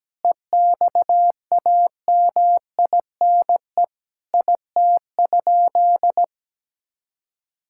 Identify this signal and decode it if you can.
Morse code, heard as a keyed tone, about 17 wpm, 700 Hz. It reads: EXAMINE IT?
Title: Sentences from Top 1600 Words 17wpm